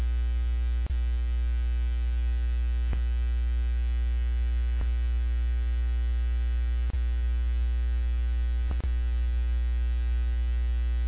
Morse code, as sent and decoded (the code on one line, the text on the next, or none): none